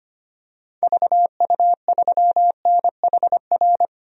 Morse code, heard as a keyed tone, about 25 wpm, 700 Hz. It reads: VU3NHR
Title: DXCC-100 Call Signs - 6 Characters - 25wpm